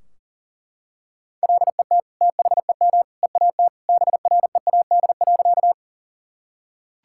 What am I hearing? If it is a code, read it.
Morse code, heard as a keyed tone, about 40 wpm, 700 Hz. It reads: LET THEM EAT BREAD.
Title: Sentences from Top 1600 Words 40wpm